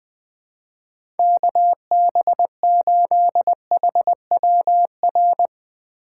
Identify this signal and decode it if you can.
Morse code, heard as a keyed tone, about 20 wpm, 700 Hz. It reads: KB8HWR
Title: USA Call Signs - 6 Characters - 6x Character Spacing - 20wpm